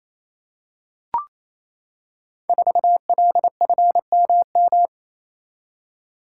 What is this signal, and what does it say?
Morse code, heard as a keyed tone, about 27 wpm, 700 Hz. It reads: 4LFMM
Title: Code Groups - 5 Letter:Numbers - 28wpm